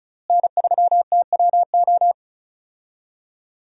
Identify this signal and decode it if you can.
Morse code, heard as a keyed tone, about 34 wpm, 700 Hz. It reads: N3TWO
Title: USA Call Signs - 5 Characters - Repeated 2x - 35wpm